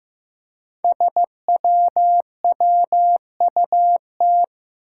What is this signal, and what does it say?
Morse code, heard as a keyed tone, about 15 wpm, 700 Hz. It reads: SWWUT
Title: Five Challenging Letter-Number Combinations 15wpm